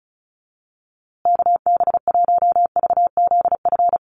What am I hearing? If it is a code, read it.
Morse code, heard as a keyed tone, about 34 wpm, 700 Hz. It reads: KB1VZF
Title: USA Call Signs Repeated 2x 35wpm